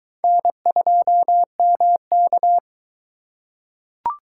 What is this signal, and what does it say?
Morse code, heard as a keyed tone, about 22 wpm, 700 Hz. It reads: N2MK